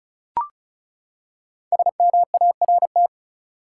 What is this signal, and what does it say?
Morse code, heard as a keyed tone, about 34 wpm, 700 Hz. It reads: SMART
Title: Five Letter Words - 35wpm